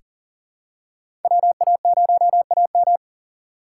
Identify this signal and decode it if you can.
Morse code, heard as a keyed tone, about 40 wpm, 700 Hz. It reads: WA0AM
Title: USA Call Signs - 4x Character Spacing - 40wpm